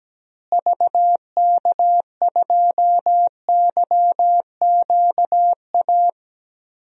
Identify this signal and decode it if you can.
Morse code, heard as a keyed tone, about 17 wpm, 700 Hz. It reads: VK2YQA
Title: DXCC-100 Call Signs - 6 Characters - Repeated 2x - 17wpm